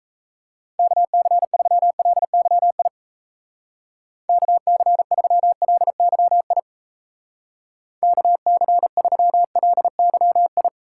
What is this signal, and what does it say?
Morse code, heard as a keyed tone, about 35 wpm, 700 Hz. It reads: KC3LYI KC3LYI KC3LYI